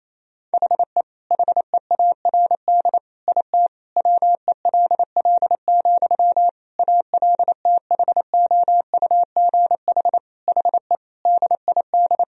Morse code, heard as a keyed tone, about 28 wpm, 690 Hz. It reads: HE HEARD IT WELL, ALTHOUGH HE DID
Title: Sentences from Top 1000 Words - Review Part 3 - 28wpm